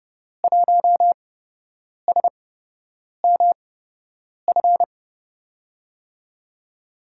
Morse code, heard as a keyed tone, about 30 wpm, spaced as for 8 wpm, 700 Hz. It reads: 1SMF
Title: USA Call Signs - 5 Characters - 8x Character Spacing - 30wpm